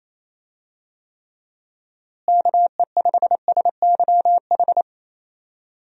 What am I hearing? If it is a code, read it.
Morse code, heard as a keyed tone, about 28 wpm, 700 Hz. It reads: KE5SYH
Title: USA Call Signs - 6 Characters - Repeated 2x - 28wpm